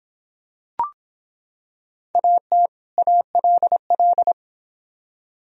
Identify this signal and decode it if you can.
Morse code, heard as a keyed tone, about 24 wpm, 700 Hz. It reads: AT ALL